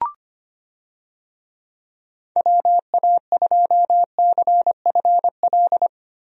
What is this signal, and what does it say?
Morse code, heard as a keyed tone, about 25 wpm, 700 Hz. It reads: WA2CFL
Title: USA Call Signs - 6 Characters - Repeated 3x - 25wpm